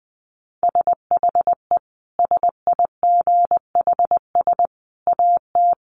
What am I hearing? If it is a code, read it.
Morse code, heard as a keyed tone, about 20 wpm, 700 Hz. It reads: SHE SIGHS AT
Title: Dit Challenge 20wpm